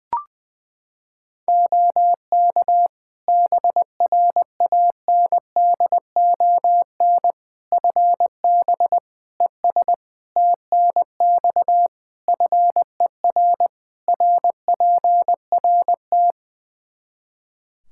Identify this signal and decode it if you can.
Morse code, heard as a keyed tone, about 20 wpm, 700 Hz. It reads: OK BRANDON FB ES TNX FER RPRT
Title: Practice QSO - Exchange 2 - Info - 20wpm